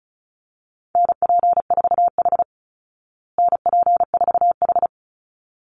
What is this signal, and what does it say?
Morse code, heard as a keyed tone, about 34 wpm, 700 Hz. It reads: NP4H NP4H
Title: Call Signs Repeated 3x - 35wpm